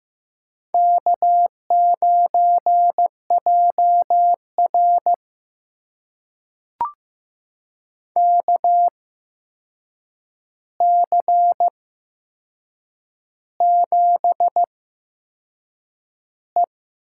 Morse code, heard as a keyed tone, about 15 wpm, spaced as for 6 wpm, 700 Hz. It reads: K9JR KC7E
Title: USA Call Signs - 8x Character Spacing - 15wpm